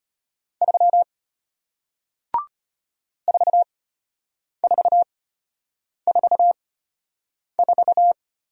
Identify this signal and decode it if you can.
Morse code, heard as a keyed tone, about 33 wpm, spaced as for 8 wpm, 700 Hz. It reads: 3 4444